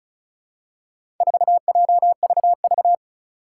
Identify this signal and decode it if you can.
Morse code, heard as a keyed tone, about 36 wpm, 700 Hz. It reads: 4JVV
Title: Four Challenging Letter-Number Combinations 35wpm